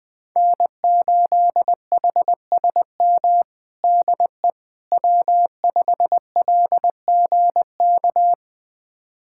Morse code, heard as a keyed tone, about 20 wpm, 700 Hz. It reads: N8HSM DE W5LGK
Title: USA Callsign DE Callsign - 5 Characters - 20wpm